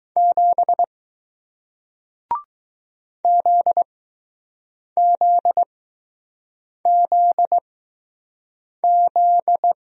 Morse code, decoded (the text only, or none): 7 ZZZZ